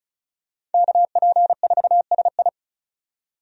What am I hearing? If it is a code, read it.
Morse code, heard as a keyed tone, about 36 wpm, 700 Hz. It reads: KP4SI
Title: USA Call Signs - 5 Characters - 35wpm